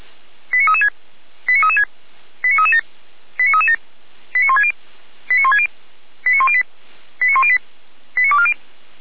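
Five-tone selective call (ZVEI), heard as quick runs of five tones, tones of about 70 ms. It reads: 89336 89336 89337 89337 89160 89160 89108 89108 89240
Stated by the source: ZVEI five-tone selective call received with an SDR.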